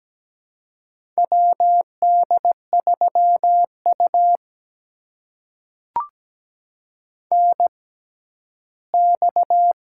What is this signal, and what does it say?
Morse code, heard as a keyed tone, about 17 wpm, spaced as for 12 wpm, 700 Hz. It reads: WD3U NX
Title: USA Call Signs - 4 Characters - 6x Character Spacing - 17wpm